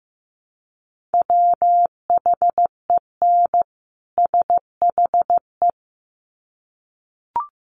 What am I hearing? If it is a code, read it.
Morse code, heard as a keyed tone, about 15 wpm, 700 Hz. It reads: WHEN SHE
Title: Sets of 2 Words 15wpm